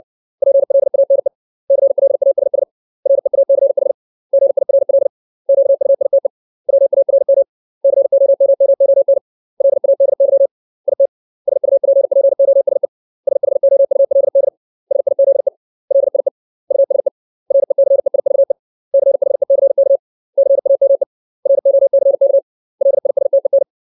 Morse code, heard as a keyed tone, about 60 wpm, 550 Hz. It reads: PLANE CLASS REACH GIRL QUITE WARM COMMON LADY IT SUPPOSE SHOULD SIZE LIE USE RECEIVE CHECK CAME ROCK LISTEN
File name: 632 words / 632 Words 60WPM